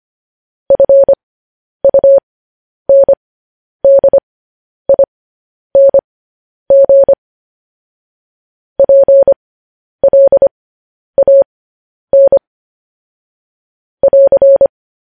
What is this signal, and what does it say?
Morse code, heard as a keyed tone, about 25 wpm, spaced as for 10 wpm, 550 Hz. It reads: FUNDING PLAN <AR>
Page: news-headlines-in-morse-code-at-10-wpm